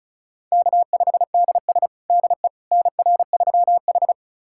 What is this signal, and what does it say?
Morse code, heard as a keyed tone, about 36 wpm, 700 Hz. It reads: K5DS DE NR3H
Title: USA Callsign DE Callsign - 4 Characters - 35wpm